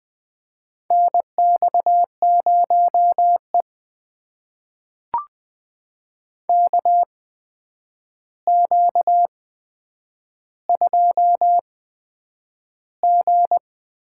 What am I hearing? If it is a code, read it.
Morse code, heard as a keyed tone, about 20 wpm, spaced as for 9 wpm, 700 Hz. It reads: NX0E KQ2G
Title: USA Call Signs - 4 Characters - 8x Character Spacing - 20wpm